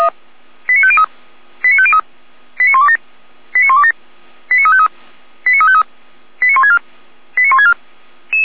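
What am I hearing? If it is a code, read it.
Five-tone selective call (ZVEI), heard as quick runs of five tones, tones of about 70 ms. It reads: 89503 89503 89127 89127 89353 89353 89164 89164
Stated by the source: ZVEI five-tone selective call received on an SDR.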